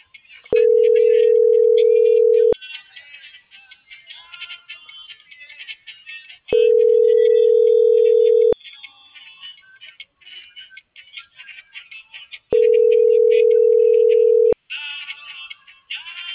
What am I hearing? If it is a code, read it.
Ringback tone (North America), 440 + 480 Hz.